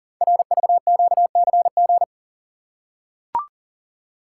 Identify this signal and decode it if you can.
Morse code, heard as a keyed tone, about 39 wpm, 700 Hz. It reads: RVQCG
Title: Code Groups - 5 Letters - 40wpm